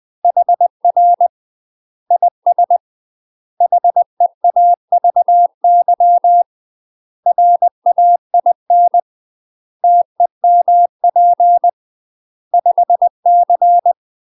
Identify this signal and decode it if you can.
Morse code, heard as a keyed tone, about 20 wpm, 700 Hz. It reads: HR IS HEAVY RAIN TEMP 5C
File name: UK MASTER QSOs TRAINING / 20WPM / UK MASTER QSOs 0041